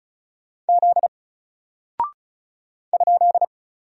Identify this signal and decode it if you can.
Morse code, heard as a keyed tone, about 36 wpm, 700 Hz. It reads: Z ?
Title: Single Character (Letter:Number:Punctuation:Prosign) Alt 35wpm